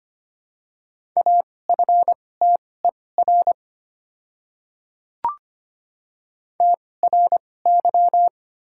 Morse code, heard as a keyed tone, about 25 wpm, spaced as for 18 wpm, 700 Hz. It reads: AFTER TRY
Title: 2x Character Spacing - Top 200 Words - 25wpm